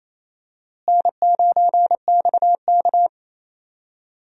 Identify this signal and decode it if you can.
Morse code, heard as a keyed tone, about 28 wpm, 700 Hz. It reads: N9XK
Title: USA Call Signs - 4 Characters - Repeated 2x - 28wpm